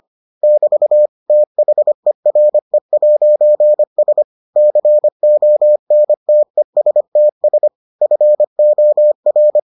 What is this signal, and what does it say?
Morse code, heard as a keyed tone, about 25 wpm, 600 Hz. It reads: <BT> THERE'S CONTESTS FOR